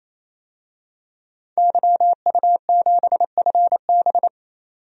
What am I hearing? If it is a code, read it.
Morse code, heard as a keyed tone, about 28 wpm, 700 Hz. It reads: YU7FB
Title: DXCC-100 Call Signs - 5 Characters - 28wpm